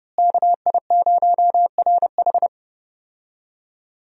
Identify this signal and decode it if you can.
Morse code, heard as a keyed tone, about 30 wpm, 700 Hz. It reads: KI0RH
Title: USA Call Signs - 5 Characters - Repeated 2x - 30wpm